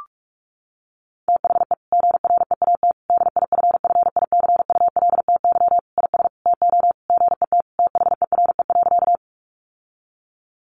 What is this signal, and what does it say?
Morse code, heard as a keyed tone, about 44 wpm, 700 Hz. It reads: THE GREAT DIFFICULTY IS TO GET THERE.
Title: Sentences from Top 1500 Words 45wpm